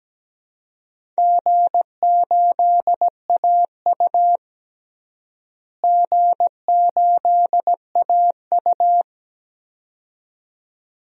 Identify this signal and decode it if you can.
Morse code, heard as a keyed tone, about 17 wpm, 700 Hz. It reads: G8AU G8AU
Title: DXCC-100 Call Signs - 4 Characters - Repeated 2x - 17wpm